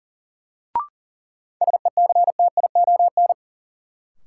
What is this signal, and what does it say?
Morse code, heard as a keyed tone, about 40 wpm, 700 Hz. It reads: SECTION
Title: Sentences from Top 1000 Words - Review Part 5 - New Word Drill - 40wpm